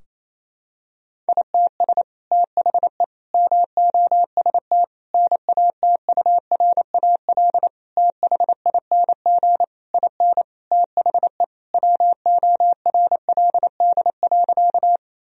Morse code, heard as a keyed tone, about 28 wpm, 700 Hz. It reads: ITS THE MOST NATURAL THING IN THE WORLD.
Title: Sentences from Top 1000 Words - Review Part 4 - 28wpm